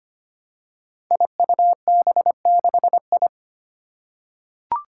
Morse code, heard as a keyed tone, about 25 wpm, 700 Hz. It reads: IUB6I
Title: Five Challenging Letter-Number Combinations 25wpm